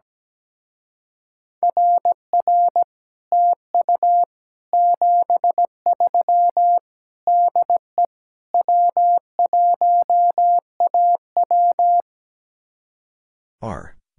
Morse code, heard as a keyed tone, about 17 wpm, 700 Hz. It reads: RR TU 73 DE W1AW
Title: Speed-Racing Common QSO Phrases - 17wpm